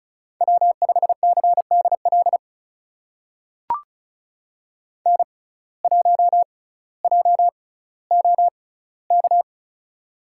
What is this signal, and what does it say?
Morse code, heard as a keyed tone, about 36 wpm, spaced as for 18 wpm, 700 Hz. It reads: W5CDL N1JOK
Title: USA Call Signs - 6x Character Spacing - 35wpm